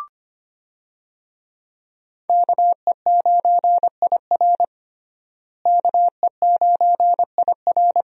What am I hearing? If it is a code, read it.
Morse code, heard as a keyed tone, about 25 wpm, 700 Hz. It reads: KE9IR KE9IR
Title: USA Call Signs - 5 Characters - Repeated 2x - 25wpm